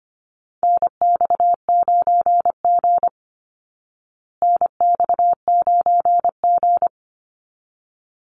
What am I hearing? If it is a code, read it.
Morse code, heard as a keyed tone, about 25 wpm, 700 Hz. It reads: NX9G NX9G
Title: Call Signs Repeated 3x - 25wpm